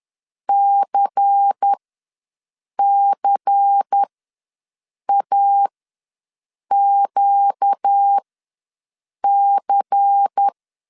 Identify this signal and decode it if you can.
Morse code, heard as a keyed tone, about 10 wpm, spaced as for 6 wpm, 800 Hz. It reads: CCAQC
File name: LEZIONI / 5 LBVC / 5 LBVC ws 10wpm 3733